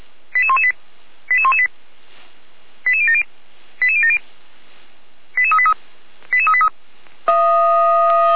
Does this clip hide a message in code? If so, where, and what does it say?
Yes: five-tone selective call (ZVEI) from 0.3 to 6.7 s, heard as quick runs of five tones, tones of about 70 ms. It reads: 88118 88118 88070 88070 88383 88383